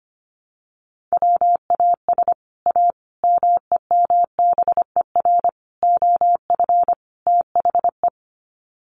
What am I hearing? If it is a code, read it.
Morse code, heard as a keyed tone, about 25 wpm, 700 Hz. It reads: WAS A MEMBER OF THE
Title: Sets of 5 Words 25wpm